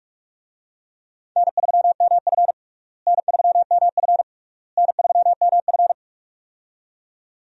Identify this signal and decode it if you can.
Morse code, heard as a keyed tone, about 46 wpm, 700 Hz. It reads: N3MF N3MF N3MF
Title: USA Call Signs - 4 Characters - Repeated 3x - 45wpm